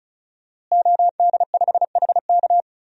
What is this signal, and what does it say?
Morse code, heard as a keyed tone, about 34 wpm, 700 Hz. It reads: OD5HK
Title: DXCC-100 Call Signs - 5 Characters - Repeated 3x - 35wpm